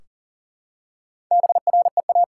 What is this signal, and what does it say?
Morse code, heard as a keyed tone, about 40 wpm, 700 Hz. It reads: BREA
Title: Sentences from Top 1600 Words 40wpm